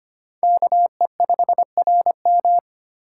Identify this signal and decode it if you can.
Morse code, heard as a keyed tone, about 31 wpm, 700 Hz. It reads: KE5RM
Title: Speed Racing - USA 2x2 Call Signs 25wpm